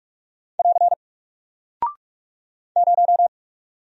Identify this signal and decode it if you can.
Morse code, heard as a keyed tone, about 44 wpm, 700 Hz. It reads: <AR> 0
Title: Single Character (Letter:Number:Punctuation:Prosign) Alt 45wpm